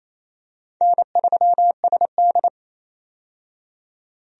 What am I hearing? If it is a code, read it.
Morse code, heard as a keyed tone, about 28 wpm, 700 Hz. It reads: N3SD
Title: USA Call Signs - 4 Characters - 28wpm